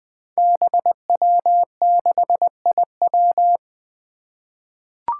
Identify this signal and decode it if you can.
Morse code, heard as a keyed tone, about 20 wpm, 700 Hz. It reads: BW6IW
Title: Five Challenging Letter-Number Combinations 20wpm